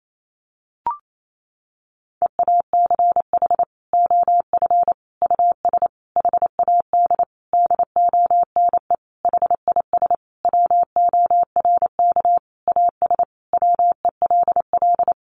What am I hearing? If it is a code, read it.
Morse code, heard as a keyed tone, about 28 wpm, 700 Hz. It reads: EACH OF US HAD DONE HIS WORK AS WELL
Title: Sentences from Top 500 Words 28wpm